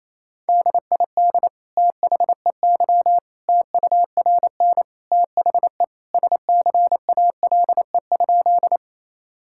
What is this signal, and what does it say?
Morse code, heard as a keyed tone, about 28 wpm, 700 Hz. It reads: DID THEY TURN THE SCALE?
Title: Sentences from Top 1000 Words - Review Part 5 - 28wpm